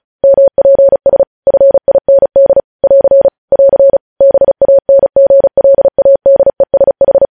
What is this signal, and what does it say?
Morse code, heard as a keyed tone, about 36 wpm, 550 Hz. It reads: MPS FIND <AR> <AR> BANGLADESH
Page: news headlines in morse code at 35 wpm